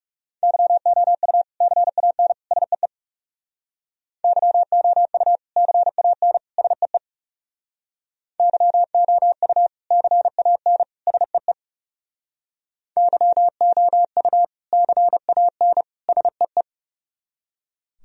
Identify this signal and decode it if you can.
Morse code, heard as a keyed tone, about 39 wpm, 700 Hz. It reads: YOU CAN SEE YOU CAN SEE YOU CAN SEE YOU CAN SEE